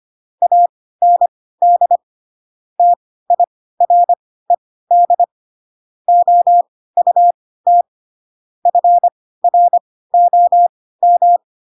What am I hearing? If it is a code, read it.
Morse code, heard as a keyed tone, about 25 wpm, spaced as for 16 wpm, 700 Hz. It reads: AND TIRED OUT FROM